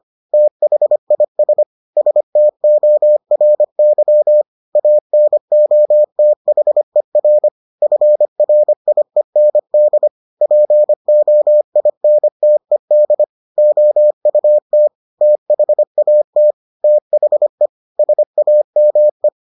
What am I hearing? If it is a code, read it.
Morse code, heard as a keyed tone, about 25 wpm, 600 Hz. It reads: THIS STORY ANOTHER FRIEND POINTED OUT THAT THE SAME